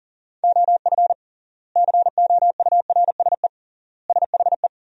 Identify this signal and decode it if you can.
Morse code, heard as a keyed tone, about 40 wpm, 700 Hz. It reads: OF COURSE SHE
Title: 3x Word-Spacing - Sentences from Top 500 Words - 40wpm